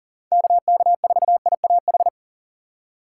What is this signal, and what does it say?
Morse code, heard as a keyed tone, about 40 wpm, 700 Hz. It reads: KK4IAH
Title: USA Call Signs - 6 Characters - 6x Character Spacing - 40wpm